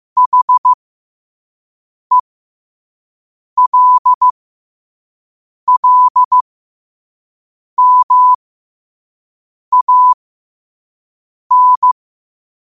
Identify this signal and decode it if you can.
Morse code, heard as a keyed tone, about 15 wpm, spaced as for 5 wpm, 1000 Hz. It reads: HELLMAN
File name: qotdfiles / 5wpm